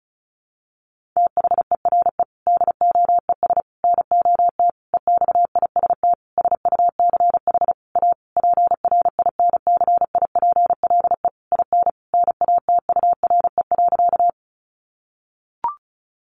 Morse code, heard as a keyed tone, about 34 wpm, 700 Hz. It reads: THERE DOES NOT EXIST SUCH A PRINCIPLE IN NATURE.
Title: Sentences from Top 1400 Words 35wpm